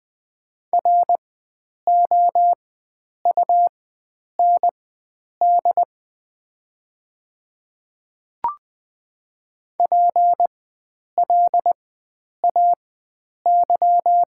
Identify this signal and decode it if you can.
Morse code, heard as a keyed tone, about 20 wpm, spaced as for 9 wpm, 700 Hz. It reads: ROUND PLAY
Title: 4x Character Spacing - Top 300 Words - 20wpm